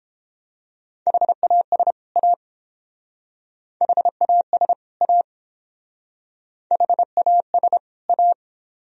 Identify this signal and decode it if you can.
Morse code, heard as a keyed tone, about 29 wpm, 700 Hz. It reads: HAS A HAS A HAS A